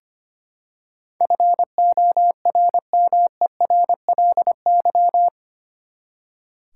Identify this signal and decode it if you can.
Morse code, heard as a keyed tone, about 25 wpm, 700 Hz. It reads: FORMERLY
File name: Eight Letter Words - 25wpm